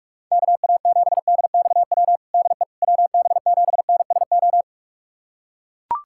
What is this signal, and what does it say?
Morse code, heard as a keyed tone, about 41 wpm, 700 Hz. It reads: KA7DXW DE WB7NSO
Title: USA Callsign DE Callsign 45wpm